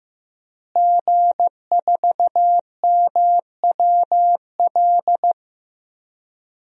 Morse code, heard as a keyed tone, about 15 wpm, 700 Hz. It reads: G4MWL
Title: DXCC-100 Call Signs - 5 Characters - Repeated 2x - 15wpm